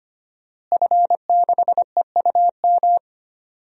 Morse code, heard as a keyed tone, about 25 wpm, 700 Hz. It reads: F6EUM
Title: DXCC-100 Call Signs - 5 Characters - Repeated 2x - 25wpm